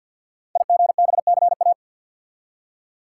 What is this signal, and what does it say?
Morse code, heard as a keyed tone, about 47 wpm, 700 Hz. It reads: IGBCA
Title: Code Groups - 5 Letters - 50wpm